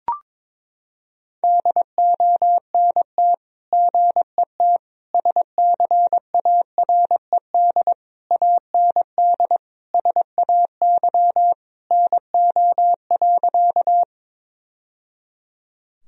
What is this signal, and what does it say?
Morse code, heard as a keyed tone, about 22 wpm, 700 Hz. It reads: DONT GET SCARED AND SAY NO.
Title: Sentences from Top 1000 Words - Review Part 4 - 22wpm